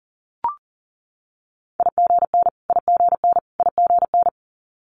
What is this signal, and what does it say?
Morse code, heard as a keyed tone, about 40 wpm, 700 Hz. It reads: IGN IGN IGN
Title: Multi-letter Phonemes Repeated 3x 40wpm